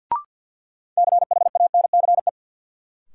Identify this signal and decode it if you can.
Morse code, heard as a keyed tone, about 53 wpm, 700 Hz. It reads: CHANCE
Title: Sets of 3 Words - New Word Drill - 50wpm